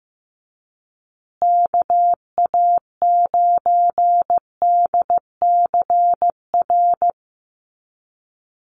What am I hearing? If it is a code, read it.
Morse code, heard as a keyed tone, about 15 wpm, 700 Hz. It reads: KA9DCR
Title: USA Call Signs Repeated 2x 15wpm